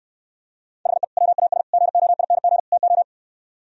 Morse code, heard as a keyed tone, about 71 wpm, 700 Hz. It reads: HE WAS GOING TO